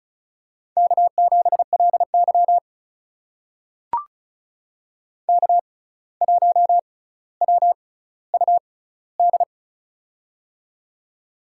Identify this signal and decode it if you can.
Morse code, heard as a keyed tone, about 36 wpm, spaced as for 17 wpm, 700 Hz. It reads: K7LY K1WUD
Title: USA Call Signs - 6x Character Spacing - 35wpm